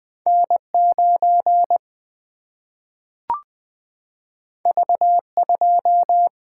Morse code, heard as a keyed tone, about 20 wpm, 700 Hz. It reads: N9 V2
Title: Single-Letter Single-Number 20wpm